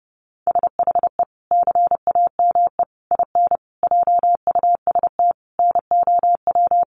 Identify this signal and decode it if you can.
Morse code, heard as a keyed tone, about 28 wpm, 700 Hz. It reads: SHE CAME IN JUST NOW
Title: Sentences from Top 300 Words 30wpm